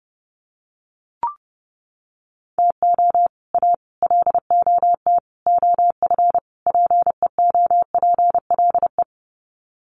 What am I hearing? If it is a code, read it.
Morse code, heard as a keyed tone, about 30 wpm, 700 Hz. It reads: TO A LOT OF PEOPLE
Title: Sets of 5 Words Encore 30wpm